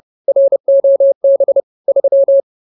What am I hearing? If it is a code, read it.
Morse code, heard as a keyed tone, about 30 wpm, 550 Hz. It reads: ROB 3